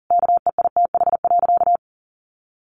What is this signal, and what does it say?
Morse code, heard as a keyed tone, about 40 wpm, 700 Hz. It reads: KEITH.
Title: A Sound Of Thunder 40wpm